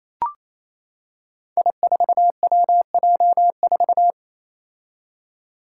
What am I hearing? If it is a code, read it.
Morse code, heard as a keyed tone, about 27 wpm, 700 Hz. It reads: I4WJ4
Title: Five Challenging Letter-Number Combinations 28wpm